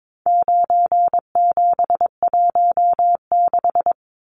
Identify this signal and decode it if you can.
Morse code, heard as a keyed tone, about 22 wpm, 700 Hz. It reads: 9716
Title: Numbers - 4-Digit 22wpm